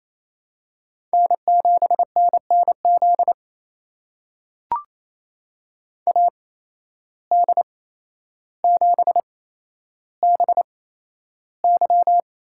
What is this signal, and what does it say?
Morse code, heard as a keyed tone, about 28 wpm, spaced as for 12 wpm, 700 Hz. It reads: N7NNZ AD7BY
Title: USA Call Signs - 5 Characters - 8x Character Spacing - 28wpm